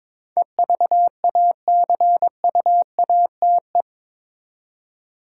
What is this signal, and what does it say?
Morse code, heard as a keyed tone, about 21 wpm, 700 Hz. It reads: EVACUATE
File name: practice / 22wpm / Eight Letter Words - 22wpm